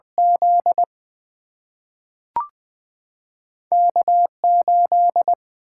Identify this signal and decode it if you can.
Morse code, heard as a keyed tone, about 20 wpm, 700 Hz. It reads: Z K8
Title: Single-Letter Single-Number 20wpm